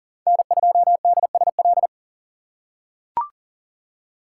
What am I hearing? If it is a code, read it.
Morse code, heard as a keyed tone, about 40 wpm, 690 Hz. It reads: N2DSL
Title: USA Call Signs - 4x Character Spacing - 40wpm